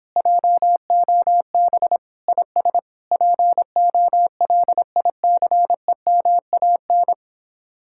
Morse code, heard as a keyed tone, about 29 wpm, 700 Hz. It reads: JOB IS POLICEMAN